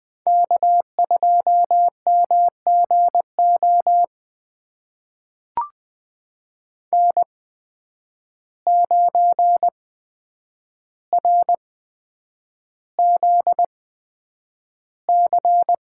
Morse code, heard as a keyed tone, about 20 wpm, spaced as for 9 wpm, 700 Hz. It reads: K2MGO N9RZC
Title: USA Call Signs - 5 Characters - 8x Character Spacing - 20wpm